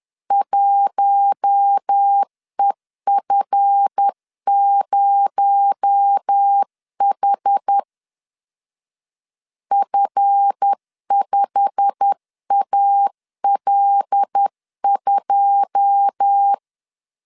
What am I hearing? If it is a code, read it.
Morse code, heard as a keyed tone, about 10 wpm, 800 Hz. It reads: 1EF0H F5AL2
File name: LEZIONI / 9 28 10wpm 2842